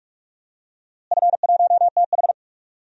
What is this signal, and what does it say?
Morse code, heard as a keyed tone, about 46 wpm, 700 Hz. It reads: F1TH
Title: DXCC-100 Call Signs - 4 Characters - Repeated 3x - 45wpm